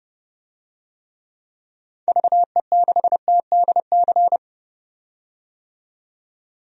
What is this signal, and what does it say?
Morse code, heard as a keyed tone, about 30 wpm, 700 Hz. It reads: VE6TDC
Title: DXCC-100 Call Signs - 6 Characters - 30wpm